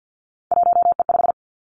Morse code, heard as a keyed tone, about 50 wpm, 700 Hz. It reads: 1E5
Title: Three Challenging Letter-Number Combinations 50wpm